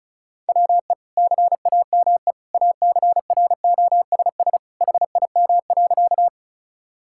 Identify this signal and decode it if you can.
Morse code, heard as a keyed tone, about 36 wpm, 700 Hz. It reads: WE CAME ACROSS HIM.
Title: Sentences from Top 1000 Words - Review Part 5 - 35wpm